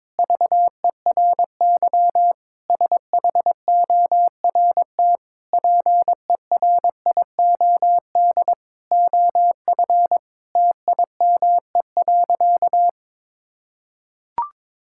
Morse code, heard as a keyed tone, about 22 wpm, 700 Hz. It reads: VERY SHORT PERIOD OF TIME.
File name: Sentences with 7-Word Phrases - 22wpm